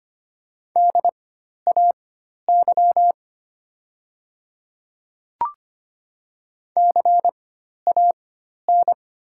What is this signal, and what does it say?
Morse code, heard as a keyed tone, about 25 wpm, spaced as for 12 wpm, 700 Hz. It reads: DAY CAN
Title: 4x Character Spacing - Top 100 Words - 25wpm